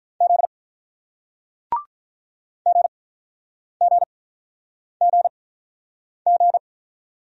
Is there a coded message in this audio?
Yes: Morse code from 0.2 to 6.6 s, heard as a keyed tone, about 47 wpm, spaced as for 9 wpm, 700 Hz. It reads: Z GGGG